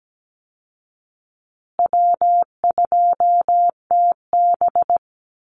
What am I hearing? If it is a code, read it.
Morse code, heard as a keyed tone, about 17 wpm, 700 Hz. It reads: W2TB
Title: Easy Call Signs Repeated 3x - 17wpm